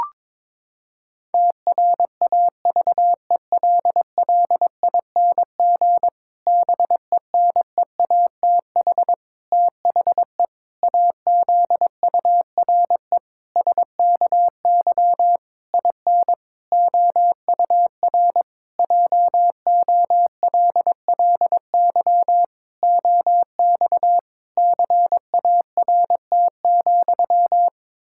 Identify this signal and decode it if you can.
Morse code, heard as a keyed tone, about 21 wpm, 700 Hz. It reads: TRAVELLING BENEATH THE AZURE SKY IN OUR JOLLY OX CART,
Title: Pangram Challenge v2 - 22wpm